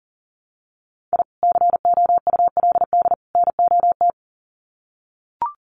Morse code, heard as a keyed tone, about 40 wpm, 700 Hz. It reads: I COULD NOT
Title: Sets of 3 Words 40wpm